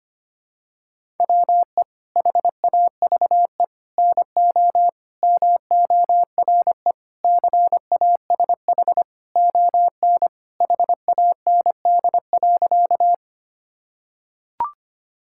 Morse code, heard as a keyed tone, about 25 wpm, 700 Hz. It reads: WE HAVE NO MORE CASH ON HAND.
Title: Sentences from Top 1000 Words - Review Part 5 - 25wpm